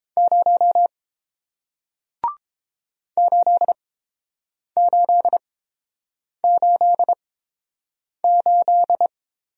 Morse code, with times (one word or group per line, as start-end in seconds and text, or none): 0.2-0.9 0
3.2-9.1 8888